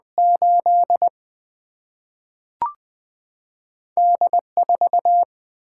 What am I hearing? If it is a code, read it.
Morse code, heard as a keyed tone, about 20 wpm, 700 Hz. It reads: 8 D4
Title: Single-Letter Single-Number 20wpm